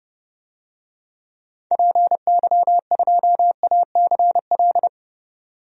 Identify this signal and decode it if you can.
Morse code, heard as a keyed tone, about 31 wpm, 700 Hz. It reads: PY2ACL
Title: DXCC-100 Call Signs - 6 Characters - 30wpm